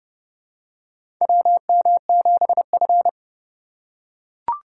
Morse code, heard as a keyed tone, about 30 wpm, 700 Hz. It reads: WM7F
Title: USA Call Signs - 4 Characters - 4x Character Spacing - 30wpm